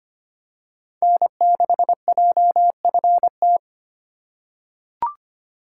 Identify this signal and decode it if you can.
Morse code, heard as a keyed tone, about 25 wpm, 700 Hz. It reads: N6JFT
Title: USA Call Signs - 5 Characters - 4x Character Spacing - 25wpm